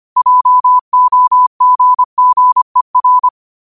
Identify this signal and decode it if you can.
Morse code, heard as a keyed tone, about 25 wpm, 1000 Hz. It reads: JOGGER
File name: qotdfiles / 25wpm